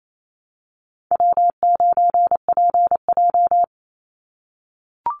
Morse code, heard as a keyed tone, about 28 wpm, 700 Hz. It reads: W9PJ
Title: Easy Call Signs - 28wpm